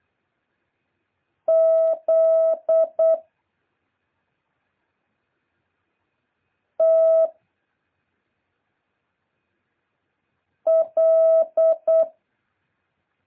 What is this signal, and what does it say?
Morse code, heard as a keyed tone, about 8 wpm, 660 Hz. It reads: Z T L
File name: jh8cbh / 2023cw